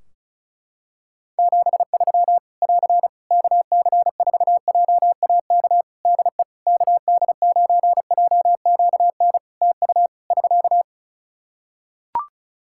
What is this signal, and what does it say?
Morse code, heard as a keyed tone, about 36 wpm, 700 Hz. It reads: 73 <AR> KC4JAK DE KD9JQN TU <SK>
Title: Practice QSO - Exchange 4 - Ending - 35wpm